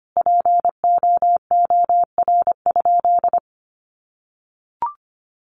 Morse code, heard as a keyed tone, about 25 wpm, 700 Hz. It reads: POOR?
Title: Sentences from Top 1100 Words 25wpm